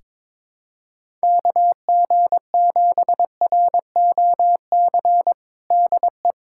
Morse code, heard as a keyed tone, about 22 wpm, 700 Hz. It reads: KG7ROC DE
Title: USA Callsign DE Callsign - 6 Characters - 22wpm